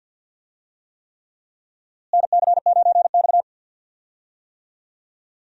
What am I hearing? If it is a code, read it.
Morse code, heard as a keyed tone, about 52 wpm, 700 Hz. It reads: NC9X
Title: USA Call Signs - 4 Characters - Repeated 3x - 50wpm